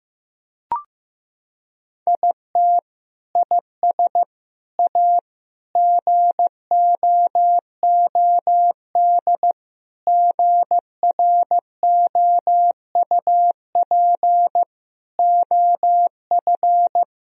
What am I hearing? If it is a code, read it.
Morse code, heard as a keyed tone, about 15 wpm, 700 Hz. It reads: IT IS A GOOD GROUP OF
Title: Sentences from Top 1000 Words - Review Part 1 - 15wpm